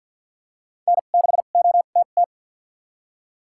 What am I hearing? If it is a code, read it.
Morse code, heard as a keyed tone, about 53 wpm, spaced as for 37 wpm, 700 Hz. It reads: N6OTT
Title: USA Call Signs - 5 Characters - 2x Character Spacing - 50wpm